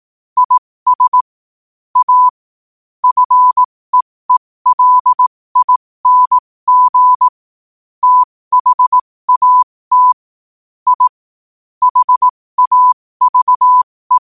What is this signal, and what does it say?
Morse code, heard as a keyed tone, about 18 wpm, spaced as for 15 wpm, 1000 Hz. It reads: IS A FEELING THAT I HAVE
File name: qotdfiles / 15wpm